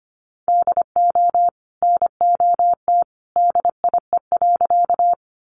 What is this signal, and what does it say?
Morse code, heard as a keyed tone, about 25 wpm, 700 Hz. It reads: DO NOT DIE.
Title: Sentences from Top 1300 Words 25wpm